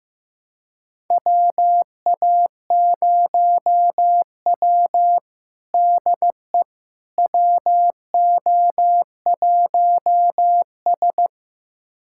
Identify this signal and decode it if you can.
Morse code, heard as a keyed tone, about 15 wpm, 700 Hz. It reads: WA0W DE WO1S
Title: USA Callsign DE Callsign - 4 Characters - 15wpm